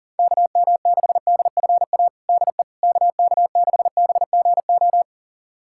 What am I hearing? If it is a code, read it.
Morse code, heard as a keyed tone, about 40 wpm, 700 Hz. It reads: KM6DFA DE KK6BGO
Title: USA Callsign DE Callsign - 6 Characters - 40wpm